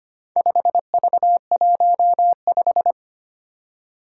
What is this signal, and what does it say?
Morse code, heard as a keyed tone, about 25 wpm, 700 Hz. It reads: 5V15
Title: Four Challenging Letter-Number Combinations 25wpm